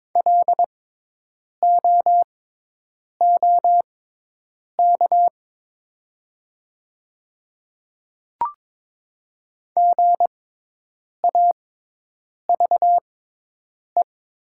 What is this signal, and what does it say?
Morse code, heard as a keyed tone, about 22 wpm, spaced as for 8 wpm, 700 Hz. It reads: LOOK GAVE
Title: 6x Character Spacing - Top 500 Words - 22wpm